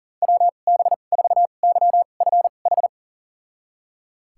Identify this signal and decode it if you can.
Morse code, heard as a keyed tone, about 40 wpm, spaced as for 29 wpm, 700 Hz. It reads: WB4YFH
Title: USA Call Signs - 2x Character Spacing - 40wpm